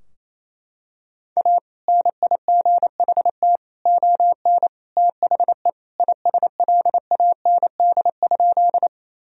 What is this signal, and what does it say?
Morse code, heard as a keyed tone, about 27 wpm, 700 Hz. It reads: A NIGHT ON THE ISLAND?
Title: Sentences from Top 1000 Words - Review Part 3 - 28wpm